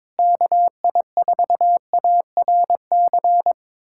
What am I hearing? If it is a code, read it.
Morse code, heard as a keyed tone, about 21 wpm, 700 Hz. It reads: KI4ARC
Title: USA Callsign DE Callsign 22wpm